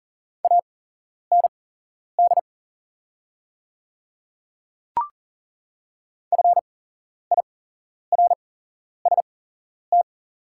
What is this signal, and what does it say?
Morse code, heard as a keyed tone, about 40 wpm, spaced as for 11 wpm, 700 Hz. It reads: AND FIRST
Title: 8x Character Spacing - Top 100 Words - 40wpm